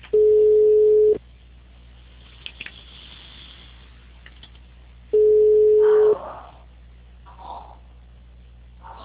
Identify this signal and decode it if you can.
Ringback tone (Europe, 425 Hz).